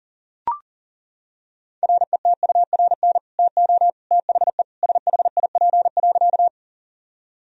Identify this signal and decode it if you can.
Morse code, heard as a keyed tone, about 40 wpm, 700 Hz. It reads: RETURN TO THE SHIP.
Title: Sentences from Top 1000 Words - Review Part 2 - 40wpm